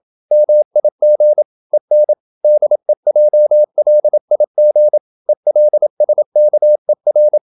Morse code, heard as a keyed tone, about 27 wpm, 600 Hz. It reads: MIG EN DEJLIG ELSKER
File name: Ferie QRQ 27wpm